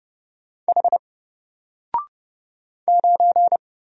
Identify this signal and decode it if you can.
Morse code, heard as a keyed tone, about 30 wpm, 700 Hz. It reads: H 9
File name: Single-Letter-Number-30wpm